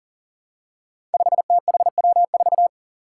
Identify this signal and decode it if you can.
Morse code, heard as a keyed tone, about 40 wpm, 700 Hz. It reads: 5THW4
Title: Five Challenging Letter-Number Combinations 40wpm